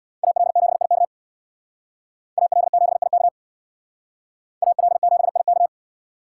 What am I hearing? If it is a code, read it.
Morse code, heard as a keyed tone, about 65 wpm, 700 Hz. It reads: AL7IK AL7IK AL7IK